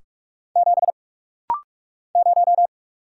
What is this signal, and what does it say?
Morse code, heard as a keyed tone, about 44 wpm, 690 Hz. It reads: 7 0
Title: Single Letter-Number - Rapid-Fire 45wpm